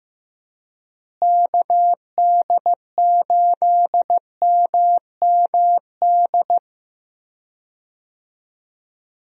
Morse code, heard as a keyed tone, about 15 wpm, 700 Hz. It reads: KD8MMD
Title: USA Call Signs - 6 Characters - Repeated 3x - 15wpm